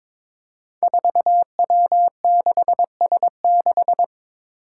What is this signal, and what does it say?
Morse code, heard as a keyed tone, about 22 wpm, 700 Hz. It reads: 4W6S6
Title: Five Challenging Letter-Number Combinations 22wpm